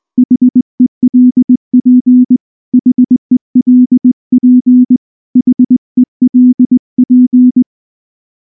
Morse code, heard as a keyed tone, about 19 wpm, 250 Hz. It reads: HELP HELP HELP